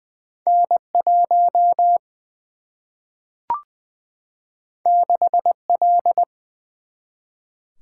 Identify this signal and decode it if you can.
Morse code, heard as a keyed tone, about 20 wpm, 700 Hz. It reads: N 1 6 L
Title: Single-Letter Single-Number 20wpm